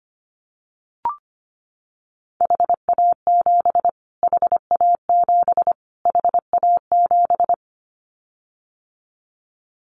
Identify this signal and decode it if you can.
Morse code, heard as a keyed tone, about 25 wpm, 700 Hz. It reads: HA7 HA7 HA7
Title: Callsign Prefixes for Top 25 DXCC - Repeated 3x - 25wpm